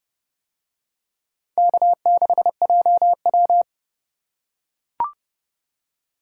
Morse code, heard as a keyed tone, about 31 wpm, 700 Hz. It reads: K6JW